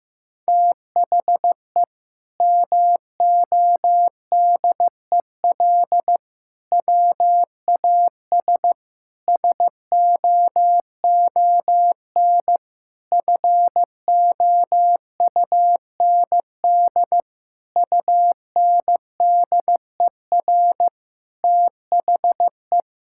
Morse code, heard as a keyed tone, about 15 wpm, 700 Hz. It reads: THE MODEL WAS SOON FOUND UNDER THE
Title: Sentences from Top 1000 Words - Review Part 5 - 15wpm